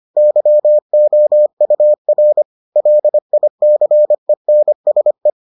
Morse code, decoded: YOUR LICENSE

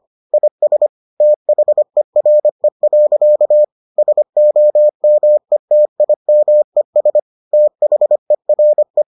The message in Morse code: IS THERE. SOMETIMES THERE

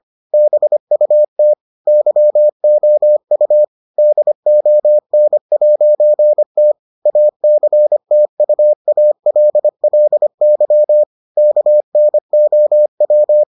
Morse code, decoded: BUT YOU DON'T ACTUALLY KNOW

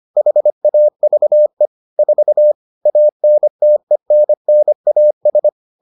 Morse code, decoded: HAVE 4 ANTENNAS

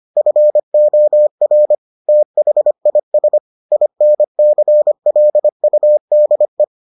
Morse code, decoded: FOR THIS INCLUDE

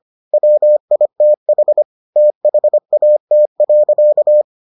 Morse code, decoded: WITH THAT.

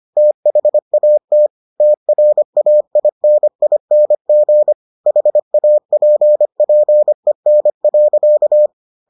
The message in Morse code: THAT TRAINING HAPPEN.

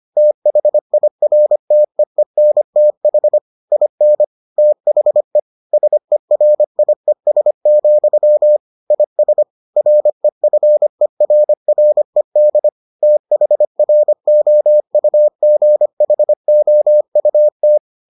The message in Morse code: THIRTEENTH IN THE SERIES, IS REFERRED THROUGHOUT

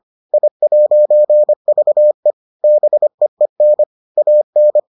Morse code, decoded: I'VE BEEN AN

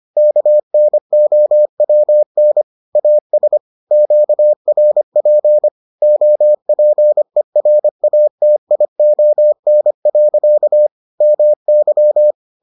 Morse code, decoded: KNOWN AS QRP OPERATION. MY

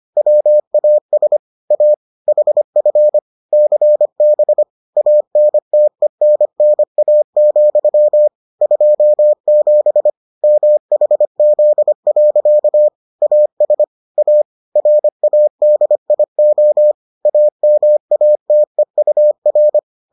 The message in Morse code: WAS A HF CB ANTENNA, 27 MHZ. AS A RADIO AMATEUR

